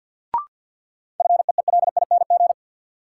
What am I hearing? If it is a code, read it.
Morse code, heard as a keyed tone, about 53 wpm, 700 Hz. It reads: FEELING
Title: Sentences from Top 1000 Words - Review Part 3 - New Word Drill - 50wpm